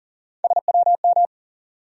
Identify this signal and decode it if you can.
Morse code, heard as a keyed tone, about 40 wpm, 700 Hz. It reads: SWM
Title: USA Call Sign Postfixes - 3 Letters 40wpm